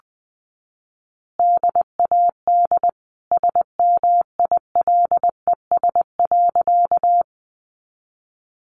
Morse code, heard as a keyed tone, about 20 wpm, 700 Hz. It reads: DAD SMILES.
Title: Easy Phrases 20wpm